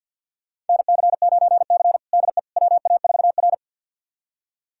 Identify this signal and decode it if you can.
Morse code, heard as a keyed tone, about 46 wpm, 700 Hz. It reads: NC9C DE WA4R
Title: USA Callsign DE Callsign - 4 Characters - 50wpm